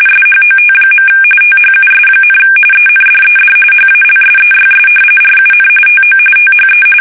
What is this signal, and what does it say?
RTTY, about 45.45 baud. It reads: SHOT. THUS, ENDED HALSEYS RUN TO THE
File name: fsk850 / battleshipnj